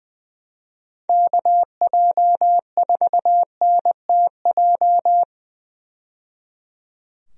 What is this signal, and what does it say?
Morse code, heard as a keyed tone, about 20 wpm, 700 Hz. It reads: KJ4NTJ
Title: USA Call Signs - 6 Characters - 20wpm